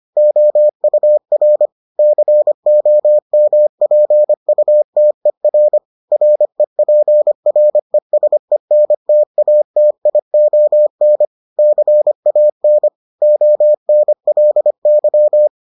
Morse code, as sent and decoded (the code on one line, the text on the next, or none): --- ..- .-. / -.-. --- -- .--. ..- - . .-. / .-. .--. .-. ... -. - .- - .. --- -. / -.-. .- -. / --- -. .-.. -.--
OUR COMPUTER REPRESENTATION CAN ONLY